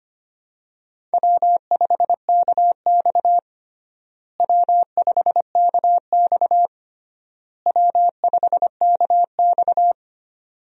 Morse code, heard as a keyed tone, about 25 wpm, 700 Hz. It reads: W5KX W5KX W5KX
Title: USA Call Signs - 4 Characters - Repeated 3x - 25wpm